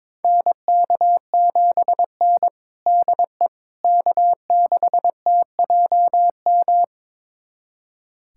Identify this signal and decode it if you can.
Morse code, heard as a keyed tone, about 22 wpm, 700 Hz. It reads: NK7N DE K6TJM
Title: USA Callsign DE Callsign 22wpm